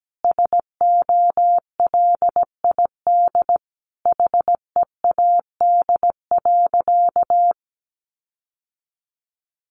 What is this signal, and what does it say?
Morse code, heard as a keyed tone, about 17 wpm, 700 Hz. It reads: SOLID HEAD.
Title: Sentences from Top 1500 Words 17wpm